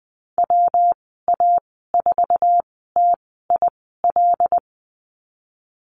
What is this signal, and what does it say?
Morse code, heard as a keyed tone, about 20 wpm, spaced as for 15 wpm, 700 Hz. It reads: WA4TIL
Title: 122 Lesson 20g - Number 3 - Call Signs 20wpm